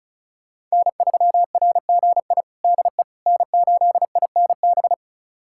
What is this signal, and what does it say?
Morse code, heard as a keyed tone, about 34 wpm, 700 Hz. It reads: N3RGI DE N8INB
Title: USA Callsign DE Callsign - 5 Characters - 35wpm